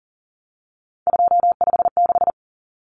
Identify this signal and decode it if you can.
Morse code, heard as a keyed tone, about 40 wpm, 700 Hz. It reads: 256
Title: Common Cut Numbers - 3-Digit 40wpm